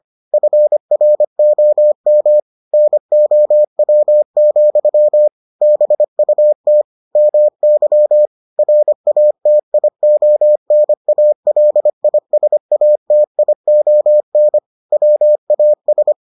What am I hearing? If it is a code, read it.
Morse code, heard as a keyed tone, about 25 wpm, 600 Hz. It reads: FROM NOW, BUT MY RATIONALISATION WAS